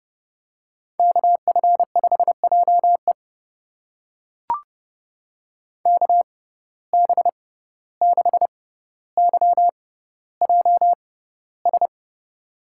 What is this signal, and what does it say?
Morse code, heard as a keyed tone, about 30 wpm, spaced as for 15 wpm, 700 Hz. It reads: KF5JE KB6YJS
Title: USA Call Signs - 6x Character Spacing - 30wpm